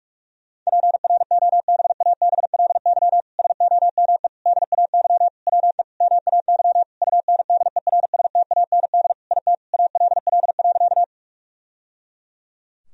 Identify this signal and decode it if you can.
Morse code, heard as a keyed tone, about 46 wpm, 700 Hz. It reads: PROBABLY SOME DAY WE MAY UNDERSTAND IT ALL.
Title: Sentences from Top 1000 Words - Review Part 4 - 45wpm